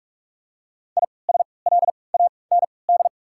Morse code, heard as a keyed tone, about 43 wpm, 700 Hz. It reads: I S L A N D
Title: 3x Character Spacing - Top 500 Words - 45wpm